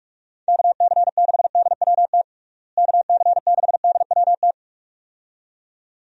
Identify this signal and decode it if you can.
Morse code, heard as a keyed tone, about 42 wpm, 700 Hz. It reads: KC6DWT KC6DWT
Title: USA Call Signs - 6 Characters - Repeated 2x - 45wpm